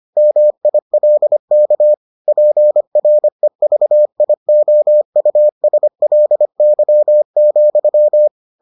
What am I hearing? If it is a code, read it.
Morse code, heard as a keyed tone, about 25 wpm, 600 Hz. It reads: MILK PREVIOUSLY,